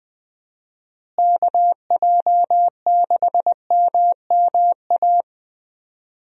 Morse code, heard as a keyed tone, about 20 wpm, 700 Hz. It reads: KJ6MMA
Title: USA Call Signs - 6 Characters - Repeated 3x - 20wpm